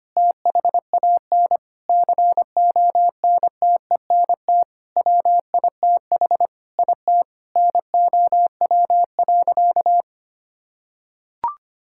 Morse code, heard as a keyed tone, about 25 wpm, 700 Hz. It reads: THAN CONTENT WITH IT NOW.
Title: Sentences from Top 1000 Words - Review Part 6 - 25wpm